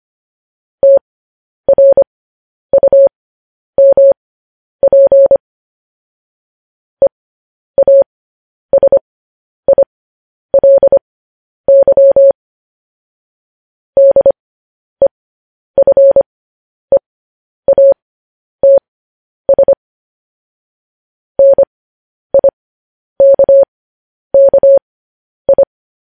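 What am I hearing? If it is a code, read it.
Morse code, heard as a keyed tone, about 25 wpm, spaced as for 10 wpm, 550 Hz. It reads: TRUMP EASILY DEFEATS NIKKI
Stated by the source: Morse code news headlines transmission.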